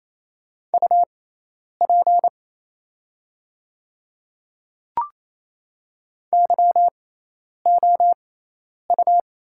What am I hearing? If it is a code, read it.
Morse code, heard as a keyed tone, about 28 wpm, spaced as for 10 wpm, 690 Hz. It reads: UP YOU
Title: 6x Character Spacing - Top 100 Words - 28wpm